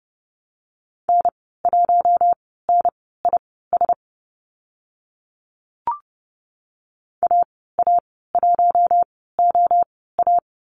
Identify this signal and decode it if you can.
Morse code, heard as a keyed tone, about 30 wpm, spaced as for 17 wpm, 700 Hz. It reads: N1NIS AA1OA
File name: Lesson 14g - Number 2 - Call Signs 30wpm